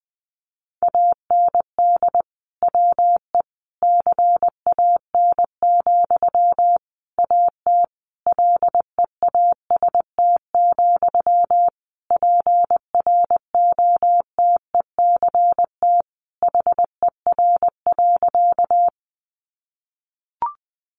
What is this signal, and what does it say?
Morse code, heard as a keyed tone, about 20 wpm, 700 Hz. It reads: AND WE CAN, AT LEAST, PROTECT HER.
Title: Sentences from Top 900 Words 20wpm